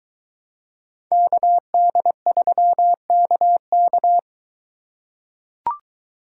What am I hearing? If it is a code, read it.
Morse code, heard as a keyed tone, about 22 wpm, 700 Hz. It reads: KD3KK